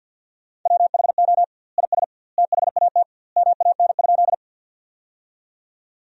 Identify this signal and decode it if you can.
Morse code, heard as a keyed tone, about 50 wpm, 700 Hz. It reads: WHO IS THAT MAN?
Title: 2x Word-Spacing - Sentences from Top 200 Words - 50wpm